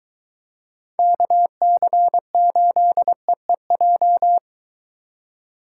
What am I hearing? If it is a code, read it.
Morse code, heard as a keyed tone, about 23 wpm, 700 Hz. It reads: KC8EEJ